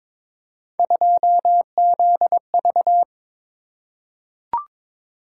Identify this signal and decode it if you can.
Morse code, heard as a keyed tone, about 22 wpm, 700 Hz. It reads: 2ZV
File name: practice / 22wpm / Three Letter-Number Combinations - 22wpm